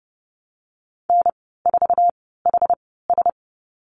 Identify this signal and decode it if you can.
Morse code, heard as a keyed tone, about 30 wpm, spaced as for 17 wpm, 700 Hz. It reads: N4HS
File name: Lesson 14g - Number 2 - Call Signs 30wpm